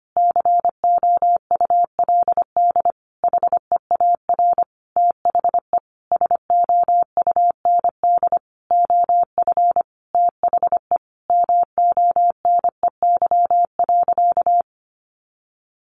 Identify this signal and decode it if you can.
Morse code, heard as a keyed tone, about 25 wpm, 700 Hz. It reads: COULD HEAR THE SOUND OF THE MONEY.
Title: Sentences from Top 1100 Words 25wpm